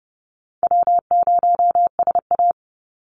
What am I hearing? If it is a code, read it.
Morse code, heard as a keyed tone, about 30 wpm, 700 Hz. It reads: W0SA
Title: Easy Call Signs - 30wpm